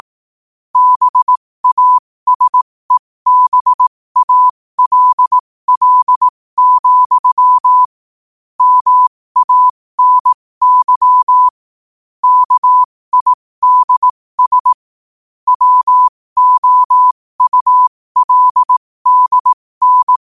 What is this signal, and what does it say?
Morse code, heard as a keyed tone, about 18 wpm, spaced as for 15 wpm, 1000 Hz. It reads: BASEBALL, MANY KIDS WOULDN
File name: qotdfiles / 15wpm